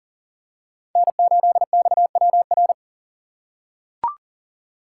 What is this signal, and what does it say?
Morse code, heard as a keyed tone, about 40 wpm, 700 Hz. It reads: N8XWR
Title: USA Call Signs - 5 Characters - 4x Character Spacing - 40wpm